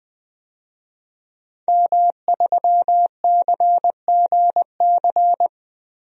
Morse code, heard as a keyed tone, about 20 wpm, 700 Hz. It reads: M3CGC
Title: DXCC-100 Call Signs - 5 Characters - 20wpm